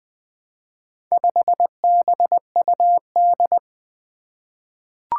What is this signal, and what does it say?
Morse code, heard as a keyed tone, about 20 wpm, 700 Hz. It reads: 5BUD
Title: Four Challenging Letter-Number Combinations 20wpm